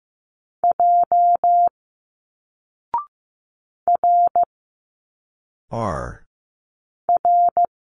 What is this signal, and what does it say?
Morse code, heard as a keyed tone, about 15 wpm, 700 Hz. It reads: J R R
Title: Single Letters 15wpm